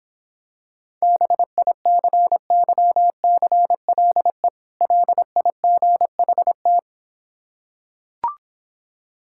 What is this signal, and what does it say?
Morse code, heard as a keyed tone, about 26 wpm, 700 Hz. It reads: BICYCLE LIGHT